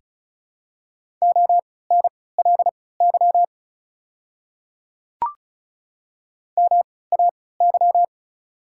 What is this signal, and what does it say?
Morse code, heard as a keyed tone, about 36 wpm, spaced as for 20 wpm, 690 Hz. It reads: ONLY MAY
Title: 3x Character Spacing - Top 200 Words - 35wpm